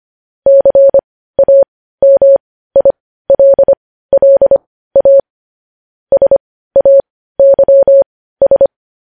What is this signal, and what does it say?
Morse code, heard as a keyed tone, about 25 wpm, spaced as for 15 wpm, 550 Hz. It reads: CAMILLA SAYS
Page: News Headlines in Morse Code at 15 WPM